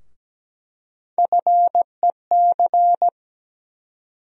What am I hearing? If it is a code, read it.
Morse code, heard as a keyed tone, about 17 wpm, 700 Hz. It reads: FEC
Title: USA Call Sign Postfixes - 3 Letters 17wpm